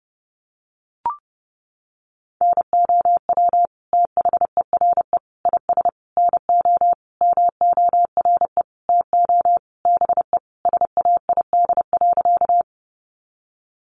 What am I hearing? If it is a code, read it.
Morse code, heard as a keyed tone, about 30 wpm, 700 Hz. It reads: NOW THERE IS NO MORE TO BE SAID.
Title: Sentences from Top 100 Words 30wpm